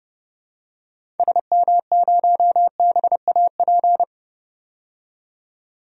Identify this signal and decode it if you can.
Morse code, heard as a keyed tone, about 30 wpm, 700 Hz. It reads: SM0BAP
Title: DXCC-100 Call Signs - 6 Characters - 30wpm